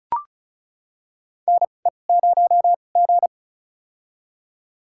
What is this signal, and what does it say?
Morse code, heard as a keyed tone, about 34 wpm, spaced as for 25 wpm, 700 Hz. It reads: NE0G
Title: USA Call Signs - 4 Characters - 2x Character Spacing - 35wpm